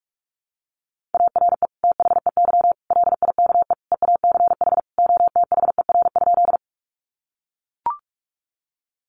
Morse code, heard as a keyed tone, about 44 wpm, 700 Hz. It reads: ARE THEY LIKE EACH OTHER?
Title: Sentences from Top 200 Words 45wpm